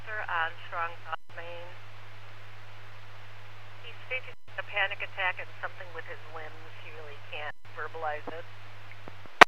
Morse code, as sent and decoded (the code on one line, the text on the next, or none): none